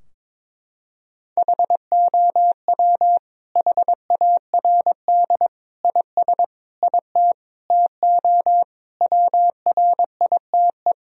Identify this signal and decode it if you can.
Morse code, heard as a keyed tone, about 22 wpm, 700 Hz. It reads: HOW HARD IS IT TO WRITE
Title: Sentences from Top 1000 Words - Review Part 3 - 22wpm